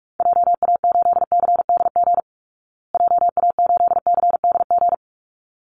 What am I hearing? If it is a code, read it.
Morse code, heard as a keyed tone, about 44 wpm, 700 Hz. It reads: JA8CDG JA8CDG
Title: Call Signs Repeated 3x - 45wpm